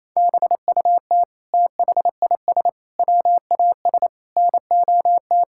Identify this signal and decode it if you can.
Morse code, heard as a keyed tone, about 28 wpm, 700 Hz. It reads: BUT THIS WAS NOT
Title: Sentences from Top 1000 Words - Review Part 6 - 28wpm